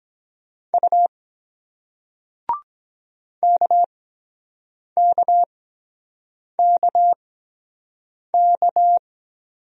Morse code, heard as a keyed tone, about 23 wpm, spaced as for 7 wpm, 700 Hz. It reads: U KKKK